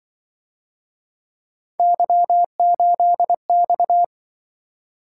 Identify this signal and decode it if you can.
Morse code, heard as a keyed tone, about 26 wpm, 700 Hz. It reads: Y8X